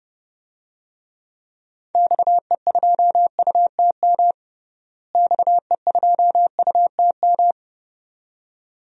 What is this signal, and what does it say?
Morse code, heard as a keyed tone, about 32 wpm, 700 Hz. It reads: XE2UTM XE2UTM
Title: DXCC-100 Call Signs - 6 Characters - Repeated 2x - 30wpm